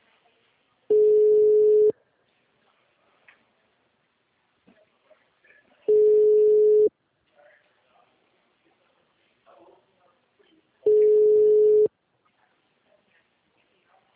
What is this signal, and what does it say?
Ringback tone (Europe, 425 Hz).
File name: out- -2015- - - .59580